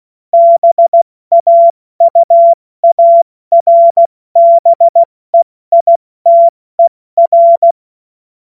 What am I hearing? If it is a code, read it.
Morse code, heard as a keyed tone, about 16 wpm, 680 Hz. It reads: BAUARBEITER
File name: Morsekurs / Witze / Witz 06. Januar 2026 70 F80